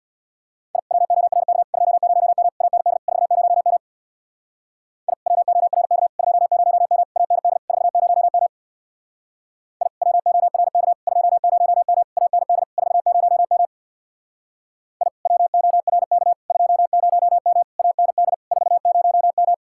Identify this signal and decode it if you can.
Morse code, heard as a keyed tone, about 62 wpm, 700 Hz. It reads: I WORK 20M AND 40M I WORK 20M AND 40M I WORK 20M AND 40M I WORK 20M AND 40M
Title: Speed-Racing Common QSO Phrases Encore - 50wpm